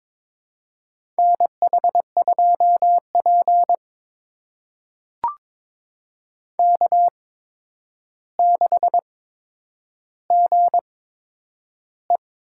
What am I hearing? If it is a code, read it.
Morse code, heard as a keyed tone, about 21 wpm, spaced as for 9 wpm, 700 Hz. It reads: NH2P K6GE
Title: USA Call Signs - 4 Characters - 8x Character Spacing - 22wpm